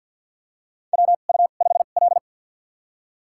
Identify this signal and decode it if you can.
Morse code, heard as a keyed tone, about 53 wpm, spaced as for 37 wpm, 700 Hz. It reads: WU5L
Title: USA Call Signs - 4 Characters - 2x Character Spacing - 50wpm